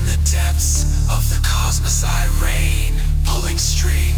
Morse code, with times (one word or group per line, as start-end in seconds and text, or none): none